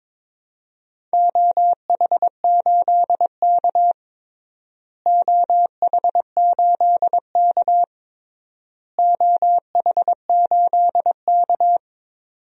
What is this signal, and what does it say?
Morse code, heard as a keyed tone, about 22 wpm, 700 Hz. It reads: OH8K OH8K OH8K
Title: DXCC-100 Call Signs - 4 Characters - Repeated 3x - 22wpm